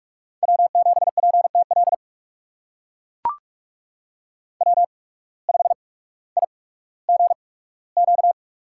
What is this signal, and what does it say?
Morse code, heard as a keyed tone, about 42 wpm, 690 Hz. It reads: W 7 P T L W 5 I G Q
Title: USA Call Signs - 8x Character Spacing - 45wpm